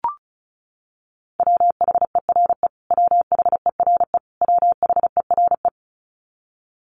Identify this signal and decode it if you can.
Morse code, heard as a keyed tone, about 34 wpm, 700 Hz. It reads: WHERE WHERE WHERE
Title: Top 500 Words Repeated 35wpm